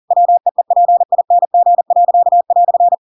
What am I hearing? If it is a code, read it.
Morse code, heard as a keyed tone, about 40 wpm, 700 Hz. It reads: WEEPING."
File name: Pinocchio / Pinocchio ch7 40 40 ik1hge